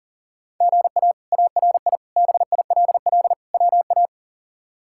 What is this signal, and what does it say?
Morse code, heard as a keyed tone, about 40 wpm, 700 Hz. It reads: GA ARI BILL WA
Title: K1USN SST QSO Practice - 40wpm